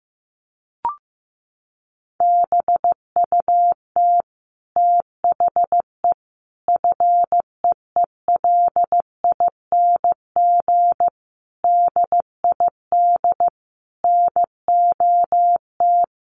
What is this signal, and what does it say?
Morse code, heard as a keyed tone, about 15 wpm, 700 Hz. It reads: BUT THE FEELING DID NOT
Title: Sentences from Top 1200 Words 15wpm